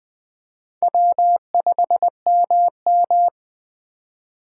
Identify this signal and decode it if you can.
Morse code, heard as a keyed tone, about 20 wpm, 700 Hz. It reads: W5MM